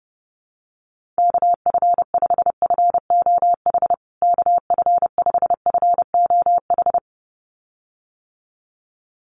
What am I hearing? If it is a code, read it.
Morse code, heard as a keyed tone, about 30 wpm, 700 Hz. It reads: KF5FOH KF5FOH
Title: USA Call Signs Repeated 2x 30wpm